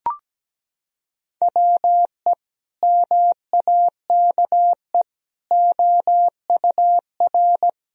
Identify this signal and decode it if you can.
Morse code, heard as a keyed tone, about 17 wpm, 700 Hz. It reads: WE MAKE OUR
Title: Sentences from Top 1000 Words - Review Part 2 - 17wpm